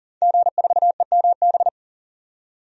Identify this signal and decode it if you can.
Morse code, heard as a keyed tone, about 38 wpm, 700 Hz. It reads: G4EMB
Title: DXCC-100 Call Signs - 5 Characters - 40wpm